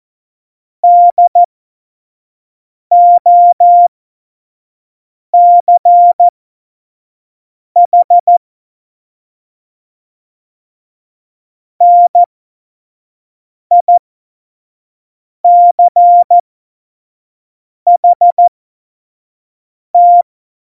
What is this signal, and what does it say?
Morse code, heard as a keyed tone, about 14 wpm, spaced as for 5 wpm, 700 Hz. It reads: DOCH NICHT
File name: Morsekurs / Witze / Witz 29. Dezember 2020, 25 F70